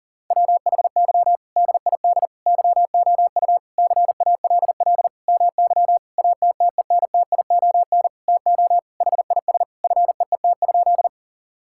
Morse code, heard as a keyed tone, about 40 wpm, 700 Hz. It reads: WHY DID YOU CALL MY ATTENTION TO HIS FEET?
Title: Sentences from Top 1000 Words - Review Part 6 - 40wpm